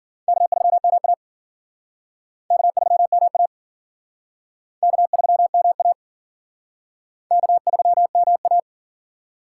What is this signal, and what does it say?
Morse code, heard as a keyed tone, about 51 wpm, 700 Hz. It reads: K3MA K3MA K3MA K3MA